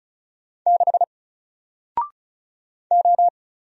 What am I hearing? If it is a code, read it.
Morse code, heard as a keyed tone, about 35 wpm, 700 Hz. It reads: 6 O
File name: Single-Letter-Number-35wpm